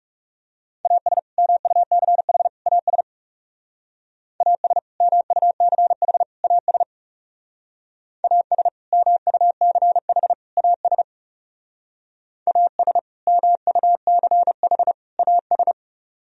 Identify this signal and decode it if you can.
Morse code, heard as a keyed tone, about 37 wpm, 700 Hz. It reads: AS MUCH AS AS MUCH AS AS MUCH AS AS MUCH AS
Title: Speed-Racing - Sets of 3 Words 30wpm